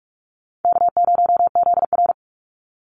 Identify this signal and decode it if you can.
Morse code, heard as a keyed tone, about 44 wpm, 700 Hz. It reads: K0ZR
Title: Easy Call Signs Repeated 3x - 45wpm